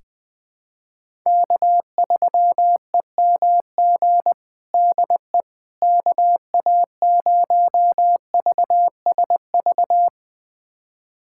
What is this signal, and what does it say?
Morse code, heard as a keyed tone, about 20 wpm, 700 Hz. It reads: K3EMG DE KA0VSV
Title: USA Callsign DE Callsign 20wpm